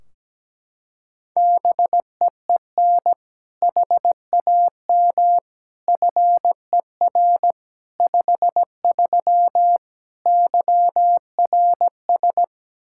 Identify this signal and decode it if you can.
Morse code, heard as a keyed tone, about 17 wpm, 700 Hz. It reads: BEEN HAM FER 53 YRS
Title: Practice QSO - Exchange 3 - More Info - 17wpm